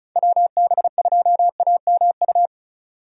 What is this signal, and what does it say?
Morse code, heard as a keyed tone, about 36 wpm, 700 Hz. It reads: WB2AMU
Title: USA Call Signs - 6 Characters - 35wpm